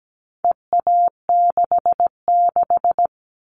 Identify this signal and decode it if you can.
Morse code, heard as a keyed tone, about 17 wpm, 700 Hz. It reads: EA66
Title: All Cut Numbers - 4-Digit 17wpm